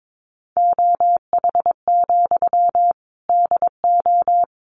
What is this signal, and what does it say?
Morse code, heard as a keyed tone, about 22 wpm, 700 Hz. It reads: OH, DO
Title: Sentences from Top 600 Words 22wpm